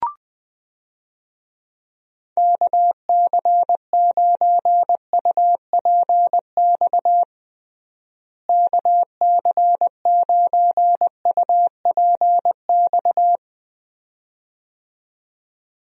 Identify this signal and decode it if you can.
Morse code, heard as a keyed tone, about 20 wpm, 700 Hz. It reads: KC9UPX KC9UPX
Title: USA Call Signs - 6 Characters - Repeated 2x - 20wpm